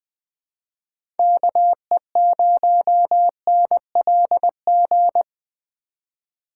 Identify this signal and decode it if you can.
Morse code, heard as a keyed tone, about 20 wpm, 700 Hz. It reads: KE0NLG
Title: USA Call Signs - 6 Characters - 2x Character Spacing - 20wpm